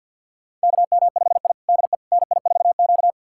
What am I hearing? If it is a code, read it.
Morse code, heard as a keyed tone, about 51 wpm, 700 Hz. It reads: KM5I DE NI4Q
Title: USA Callsign DE Callsign - 4 Characters - 50wpm